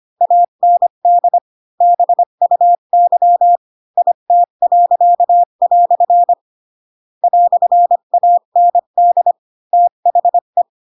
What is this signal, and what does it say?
Morse code, heard as a keyed tone, about 25 wpm, 700 Hz. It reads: AND BUY IT." "AND THE
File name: Pinocchio / Pinocchio ch8 20 25 ik1hge